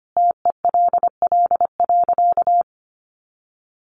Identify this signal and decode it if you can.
Morse code, heard as a keyed tone, about 25 wpm, 700 Hz. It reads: TELL.
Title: Sentences from Top 200 Words 25wpm